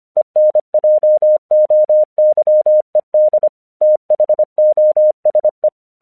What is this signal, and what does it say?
Morse code, heard as a keyed tone, about 26 wpm, 610 Hz. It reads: ENJOYED THOSE
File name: 1 English 25wpm / AQC13